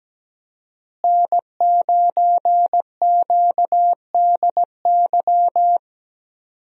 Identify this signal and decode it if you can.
Morse code, heard as a keyed tone, about 17 wpm, 700 Hz. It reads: N9QDY
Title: USA Call Signs - 5 Characters - Repeated 3x - 17wpm